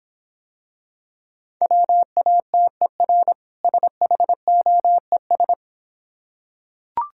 Morse code, heard as a keyed tone, about 26 wpm, 700 Hz. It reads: WATER SHOES